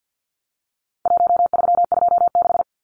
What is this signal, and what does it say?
Morse code, heard as a keyed tone, about 50 wpm, 700 Hz. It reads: 1326
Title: Numbers - 4-Digit 50wpm